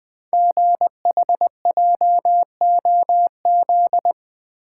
Morse code, heard as a keyed tone, about 20 wpm, 700 Hz. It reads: GHJOZ
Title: Code Groups - 5 Letters - 20wpm